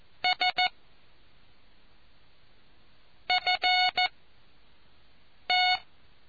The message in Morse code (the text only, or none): S F T